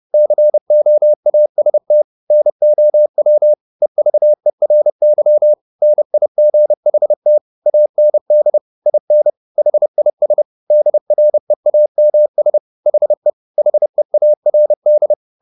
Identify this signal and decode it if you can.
Morse code, heard as a keyed tone, about 30 wpm, 600 Hz. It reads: COAST NOW EVERY NIGHT AND IN HIS DREAMS HE HEARD
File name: the old man and the sea / the old man and the sea 1 30wpm